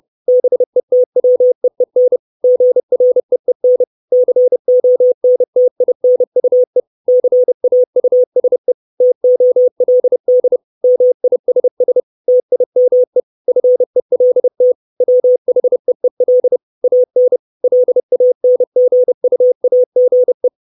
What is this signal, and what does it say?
Morse code, heard as a keyed tone, about 30 wpm, 500 Hz. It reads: BETWEEN GREEN CONTINUE CAUSE TOLD MISS TIME FELT WHEEL AN LANGUAGE
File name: 632 Words 30WPM